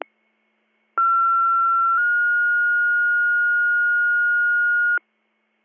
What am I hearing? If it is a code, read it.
Two-tone page, heard as a one-second tone, then a three-second tone, about 1360 then 1430 Hz. It is heard from 1.0 to 5.0 s.